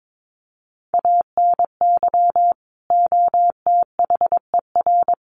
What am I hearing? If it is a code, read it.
Morse code, heard as a keyed tone, about 22 wpm, 700 Hz. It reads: ANY OTHER
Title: Sentences from Top 1100 Words 22wpm